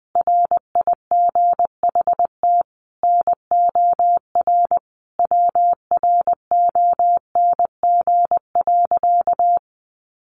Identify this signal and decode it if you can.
Morse code, heard as a keyed tone, about 19 wpm, 700 Hz. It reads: RIGHT NOR WRONG.
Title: Sentences from Top 1500 Words 20wpm